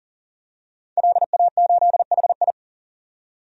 Morse code, heard as a keyed tone, about 40 wpm, 700 Hz. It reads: LA8HI
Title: DXCC-100 Call Signs - 5 Characters - 40wpm